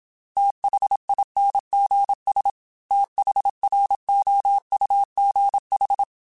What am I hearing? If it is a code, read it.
Morse code, heard as a keyed tone, about 26 wpm, 800 Hz. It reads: THINGS THROUGH
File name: TH Runs / TH Runs mp3s / TH Runs @25